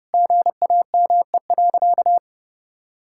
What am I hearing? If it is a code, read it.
Morse code, heard as a keyed tone, about 30 wpm, 700 Hz. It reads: GAME.
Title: Sentences from Top 1000 Words - Review Part 6 - 30wpm